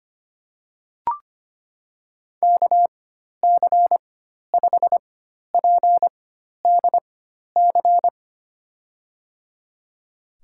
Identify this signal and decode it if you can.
Morse code, heard as a keyed tone, about 25 wpm, spaced as for 12 wpm, 700 Hz. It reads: KC5PDC
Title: USA Call Signs - 4x Character Spacing - 25wpm